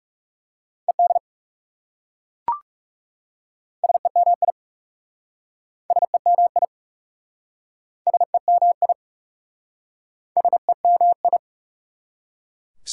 Morse code, heard as a keyed tone, about 38 wpm, 700 Hz. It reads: ED SEMI SEMI SEMI SEMI